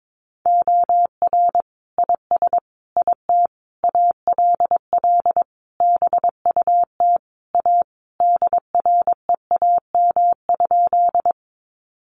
Morse code, heard as a keyed tone, about 22 wpm, 700 Hz. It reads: OR IS IT ALL BUT A DREAM?
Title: Sentences from Top 1500 Words 22wpm